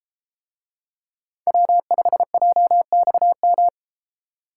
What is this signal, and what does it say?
Morse code, heard as a keyed tone, about 33 wpm, 700 Hz. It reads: W5JXM